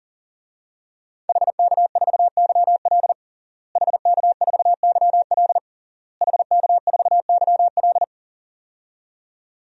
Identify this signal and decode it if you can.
Morse code, heard as a keyed tone, about 37 wpm, 700 Hz. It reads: HK4YL HK4YL HK4YL
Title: DXCC-100 Call Signs - 5 Characters - Repeated 3x - 40wpm